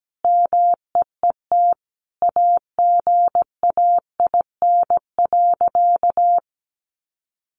Morse code, heard as a keyed tone, about 17 wpm, 700 Hz. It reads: MEET AGAIN.
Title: Sentences from Top 1100 Words 17wpm